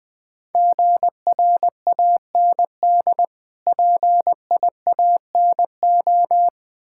Morse code, heard as a keyed tone, about 20 wpm, 700 Hz. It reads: GRAND PIANO
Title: Difficult Words - Speed-Racing with Hints - 20wpm